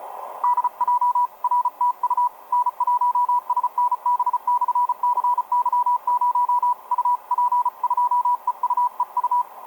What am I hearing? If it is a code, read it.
Morse code, heard as a keyed tone, about 34 wpm, 980 Hz. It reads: DJ RTU N1SNB/CY0 UP2EUEU
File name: n1snb-cy0 ea4dc etc 30m cw 06mar2011 2222